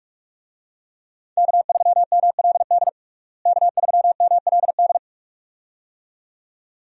Morse code, heard as a keyed tone, about 46 wpm, 700 Hz. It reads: K3MLD K3MLD
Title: USA Call Signs - 5 Characters - Repeated 2x - 45wpm